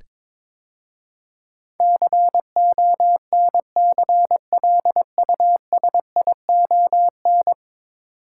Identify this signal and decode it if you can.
Morse code, heard as a keyed tone, about 22 wpm, 700 Hz. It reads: CONCLUSION
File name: practice / 22wpm / 10 Letter Words - First 3 Letters Given - 22wpm